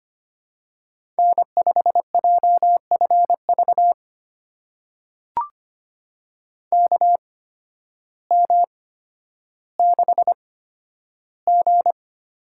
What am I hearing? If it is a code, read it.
Morse code, heard as a keyed tone, about 25 wpm, spaced as for 12 wpm, 700 Hz. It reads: N5JFV KM6G
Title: USA Call Signs - 5 Characters - 8x Character Spacing - 25wpm